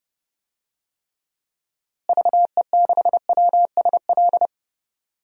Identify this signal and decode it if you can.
Morse code, heard as a keyed tone, about 30 wpm, 700 Hz. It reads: VE6WSL
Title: DXCC-100 Call Signs - 6 Characters - Repeated 2x - 30wpm